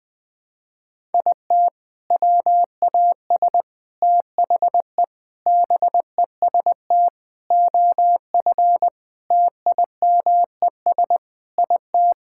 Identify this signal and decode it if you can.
Morse code, heard as a keyed tone, about 20 wpm, 700 Hz. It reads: IT WAS THE BEST OF TIMES IT
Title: Sets of 8 Words - 20wpm